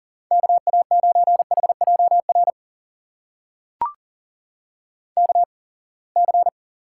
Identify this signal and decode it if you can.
Morse code, heard as a keyed tone, about 41 wpm, spaced as for 28 wpm, 700 Hz. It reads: KA9HJR KC
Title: USA Call Signs - 6 Characters - 8x Character Spacing - 40wpm